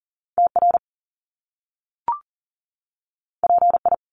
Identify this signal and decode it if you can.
Morse code, heard as a keyed tone, about 42 wpm, 700 Hz. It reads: TR PI
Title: Top Two Letter Combinations in Words 40wpm